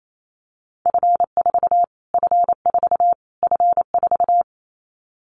Morse code, heard as a keyed tone, about 28 wpm, 700 Hz. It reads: F4 F4 F4
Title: Callsign Prefixes for Top 25 DXCC - Repeated 3x - 28wpm